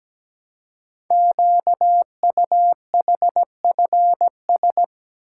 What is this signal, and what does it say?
Morse code, heard as a keyed tone, about 17 wpm, 700 Hz. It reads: QUHFS
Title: Code Groups - 5 Letters - 17wpm